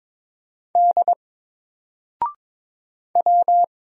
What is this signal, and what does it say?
Morse code, heard as a keyed tone, about 23 wpm, 700 Hz. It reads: D W
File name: Single-Letter-Number-22wpm